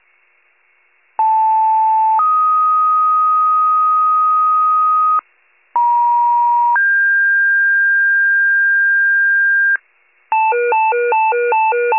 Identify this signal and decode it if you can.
Two-tone page, heard as a one-second tone, then a three-second tone, about 880 then 1250 Hz, about 950 then 1600 Hz, with noise. It is heard from 1.2 to 9.8 s.
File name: wcalert